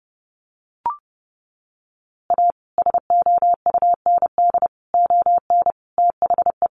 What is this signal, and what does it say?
Morse code, heard as a keyed tone, about 30 wpm, 700 Hz. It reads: A SOUND ON THE
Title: A Sound Of Thunder 30wpm